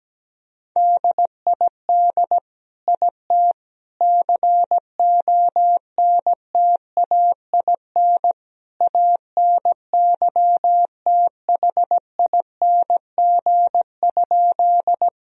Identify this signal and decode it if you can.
Morse code, heard as a keyed tone, about 17 wpm, 700 Hz. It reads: DID IT CONTAIN ANYTHING?
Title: Sentences from Top 1000 Words - Review Part 2 - 17wpm